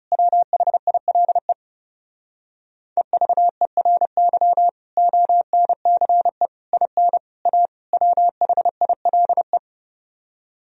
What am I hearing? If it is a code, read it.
Morse code, heard as a keyed tone, about 37 wpm, 700 Hz. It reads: WHILE EVERY ONCE IN A WHILE
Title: Speed-Racing - Sets of 5 Words 30wpm